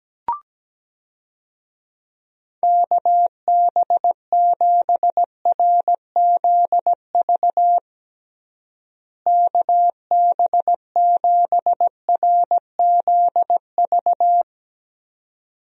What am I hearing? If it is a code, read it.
Morse code, heard as a keyed tone, about 17 wpm, 700 Hz. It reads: KB7RZV KB7RZV
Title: USA Call Signs - 6 Characters - Repeated 2x - 17wpm